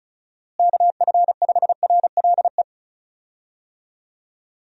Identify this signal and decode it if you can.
Morse code, heard as a keyed tone, about 34 wpm, 700 Hz. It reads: KF5RLE
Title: USA Call Signs - 6 Characters - Repeated 3x - 35wpm